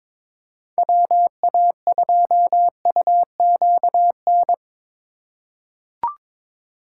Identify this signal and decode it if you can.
Morse code, heard as a keyed tone, about 23 wpm, 700 Hz. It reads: WA2UQN
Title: USA Call Signs - 6 Characters - 2x Character Spacing - 22wpm